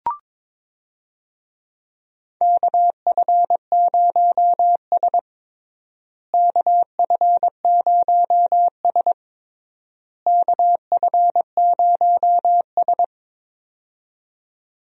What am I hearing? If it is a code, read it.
Morse code, heard as a keyed tone, about 22 wpm, 700 Hz. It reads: KF0S KF0S KF0S
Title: USA Call Signs - 4 Characters - Repeated 3x - 22wpm